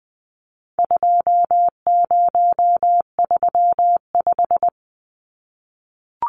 Morse code, heard as a keyed tone, about 20 wpm, 700 Hz. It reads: 2035
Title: Numbers - 4-Digit 20wpm